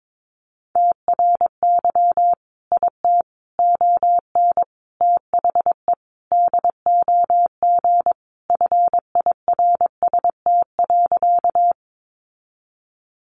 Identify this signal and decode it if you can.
Morse code, heard as a keyed tone, about 22 wpm, 700 Hz. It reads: TRY IT ON THE DOG FIRST.
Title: Sentences from Top 400 Words 22wpm